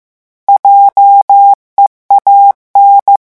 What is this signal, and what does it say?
Morse code, heard as a keyed tone, about 15 wpm, 800 Hz. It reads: JEAN
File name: morse mp3 / revision9 citations humoristiques vitesse 14 mots minute